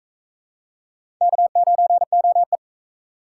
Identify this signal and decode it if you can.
Morse code, heard as a keyed tone, about 40 wpm, 700 Hz. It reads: K9OE